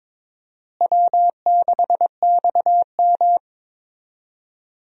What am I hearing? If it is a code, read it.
Morse code, heard as a keyed tone, about 21 wpm, 700 Hz. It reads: W6XM
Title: USA Call Signs - 4 Characters - Repeated 2x - 22wpm